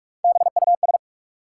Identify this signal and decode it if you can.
Morse code, heard as a keyed tone, about 44 wpm, 700 Hz. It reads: BUS
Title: Three Letter Words - 45wpm